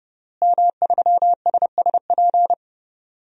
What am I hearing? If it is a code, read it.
Morse code, heard as a keyed tone, about 32 wpm, 700 Hz. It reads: M3SSP
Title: DXCC-100 Call Signs - 5 Characters - Repeated 3x - 30wpm